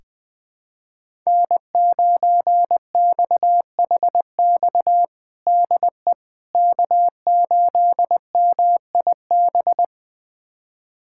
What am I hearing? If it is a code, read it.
Morse code, heard as a keyed tone, about 21 wpm, 700 Hz. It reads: N9XHX DE K8MIB
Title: USA Callsign DE Callsign - 5 Characters - 20wpm